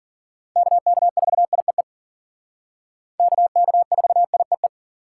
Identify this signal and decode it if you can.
Morse code, heard as a keyed tone, about 51 wpm, 700 Hz. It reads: KK4IEE KK4IEE